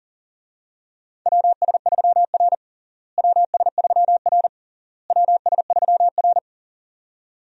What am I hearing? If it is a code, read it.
Morse code, heard as a keyed tone, about 40 wpm, 700 Hz. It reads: WS3R WS3R WS3R
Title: USA Call Signs - 4 Characters - Repeated 3x - 40wpm